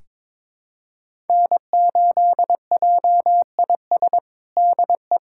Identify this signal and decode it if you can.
Morse code, heard as a keyed tone, about 22 wpm, 700 Hz. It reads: N8JIS DE
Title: USA Callsign DE Callsign 22wpm